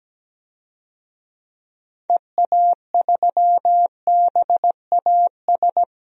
Morse code, heard as a keyed tone, about 17 wpm, 700 Hz. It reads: EA3BAS
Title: DXCC-100 Call Signs - 6 Characters - 17wpm